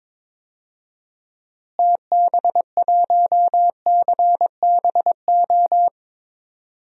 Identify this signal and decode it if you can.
Morse code, heard as a keyed tone, about 22 wpm, 700 Hz. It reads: TB1CBO
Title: DXCC-100 Call Signs - 6 Characters - Repeated 3x - 22wpm